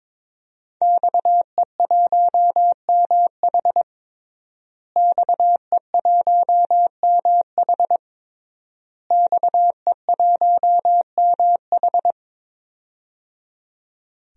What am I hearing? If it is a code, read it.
Morse code, heard as a keyed tone, about 22 wpm, 700 Hz. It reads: XE1MH XE1MH XE1MH
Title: DXCC-100 Call Signs - 5 Characters - Repeated 3x - 22wpm